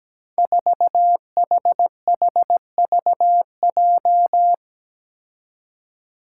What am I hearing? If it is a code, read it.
Morse code, heard as a keyed tone, about 17 wpm, 700 Hz. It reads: 4HHVJ
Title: Five Challenging Letter-Number Combinations 17wpm